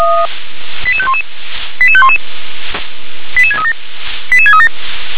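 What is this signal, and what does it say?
Five-tone selective call (ZVEI), heard as quick runs of five tones, tones of about 70 ms. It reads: 88511 88511 88637 88637
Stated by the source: ZVEI five-tone selective call received with an SDR.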